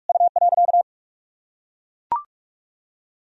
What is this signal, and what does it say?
Morse code, heard as a keyed tone, about 41 wpm, 700 Hz. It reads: U.
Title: Sentences from Top 500 Words and Abbreviations 45wpm